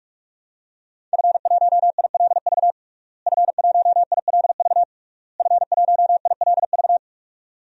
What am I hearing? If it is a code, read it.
Morse code, heard as a keyed tone, about 46 wpm, 700 Hz. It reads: F1ILV F1ILV F1ILV
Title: DXCC-100 Call Signs - 5 Characters - Repeated 3x - 45wpm